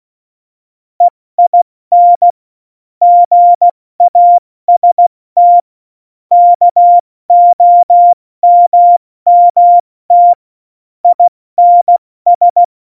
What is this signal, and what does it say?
Morse code, heard as a keyed tone, about 16 wpm, 700 Hz. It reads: EIN GAST KOMMT INS
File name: Morsekurs / Witze / Witz 16. April 2019, 70 F80